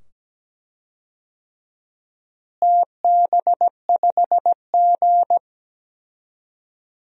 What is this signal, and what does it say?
Morse code, heard as a keyed tone, about 17 wpm, 700 Hz. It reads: TB5G
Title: DXCC-100 Call Signs - 4 Characters - Repeated 2x - 17wpm